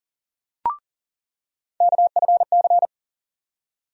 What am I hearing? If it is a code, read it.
Morse code, heard as a keyed tone, about 40 wpm, 700 Hz. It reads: KFC
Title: USA Call Sign Postfixes - 3 Letters 40wpm